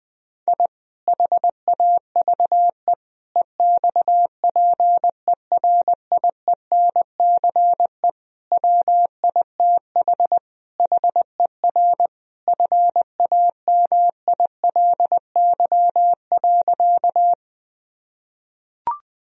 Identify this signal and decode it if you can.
Morse code, heard as a keyed tone, about 20 wpm, 700 Hz. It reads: I HAVE EXPERIENCE WITH HER FAMILY.
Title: Sentences from Top 1000 Words - Review Part 3 - 20wpm